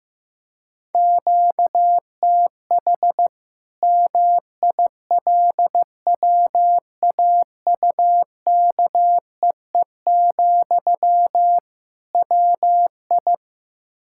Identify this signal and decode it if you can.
Morse code, heard as a keyed tone, about 15 wpm, 700 Hz. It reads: QTH MILWAUKEE, WI
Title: Practice QSO - Exchange 1 - Beginning - 15wpm